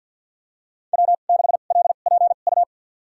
Morse code, heard as a keyed tone, about 48 wpm, spaced as for 35 wpm, 700 Hz. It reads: W6LPU
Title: USA Call Signs - 5 Characters - 2x Character Spacing - 50wpm